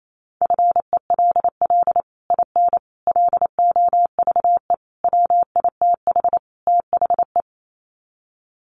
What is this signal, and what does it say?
Morse code, heard as a keyed tone, about 28 wpm, 700 Hz. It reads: FELL IN LOVE WITH THE
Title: Sets of 5 Words Encore2 28wpm